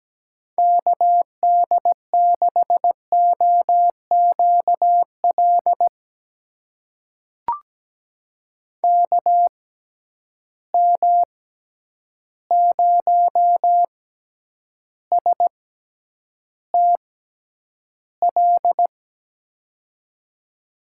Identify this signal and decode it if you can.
Morse code, heard as a keyed tone, about 17 wpm, spaced as for 9 wpm, 700 Hz. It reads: KD6OQL KM0STL
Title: USA Call Signs - 6x Character Spacing - 17wpm